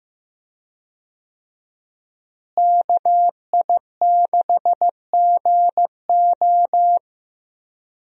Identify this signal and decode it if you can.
Morse code, heard as a keyed tone, about 15 wpm, 700 Hz. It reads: KI6GO
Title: USA Call Signs - 5 Characters - Repeated 2x - 15wpm